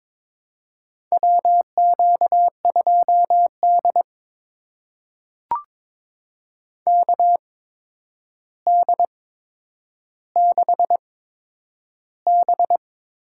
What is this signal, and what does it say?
Morse code, heard as a keyed tone, about 23 wpm, spaced as for 10 wpm, 700 Hz. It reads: WQ2D KD6B
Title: USA Call Signs - 4 Characters - 8x Character Spacing - 22wpm